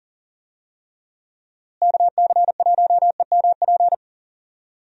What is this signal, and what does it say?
Morse code, heard as a keyed tone, about 40 wpm, 700 Hz. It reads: KC1EMP
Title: USA Call Signs - 6 Characters - Repeated 2x - 40wpm